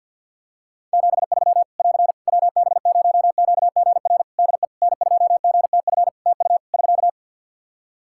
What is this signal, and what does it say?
Morse code, heard as a keyed tone, about 52 wpm, 700 Hz. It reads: 73 <AR> WB0QGR DE N2GTF TU <SK>
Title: Practice QSO - Exchange 4 - Ending - 50wpm